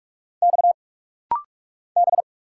Morse code, heard as a keyed tone, about 46 wpm, 700 Hz. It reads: X B
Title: Single Letters - Rapid-Fire 45wpm